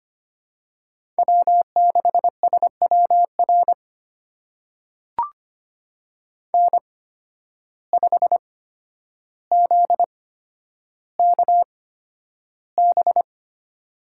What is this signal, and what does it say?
Morse code, heard as a keyed tone, about 25 wpm, spaced as for 11 wpm, 700 Hz. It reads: W6SWR N5ZKB
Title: USA Call Signs - 5 Characters - 8x Character Spacing - 25wpm